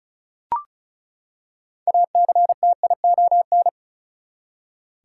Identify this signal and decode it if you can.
Morse code, heard as a keyed tone, about 34 wpm, 700 Hz. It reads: ACTION
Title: Six Letter Words - 35wpm